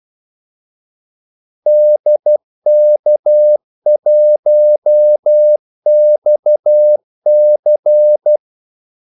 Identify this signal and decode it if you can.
Morse code, heard as a keyed tone, about 12 wpm, 600 Hz. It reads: DK1XC
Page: 2 ekstra ordmellemrum mellem kaldesignalerne